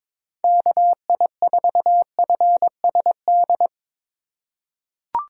Morse code, heard as a keyed tone, about 22 wpm, 700 Hz. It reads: KI4FSD
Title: USA Call Signs - 6 Characters - 4x Character Spacing - 22wpm